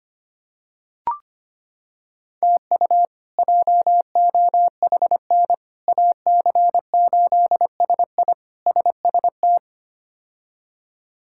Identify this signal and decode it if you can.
Morse code, heard as a keyed tone, about 25 wpm, 700 Hz. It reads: TU JOHN AC8SI SST
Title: K1USN SST QSO Practice - 25wpm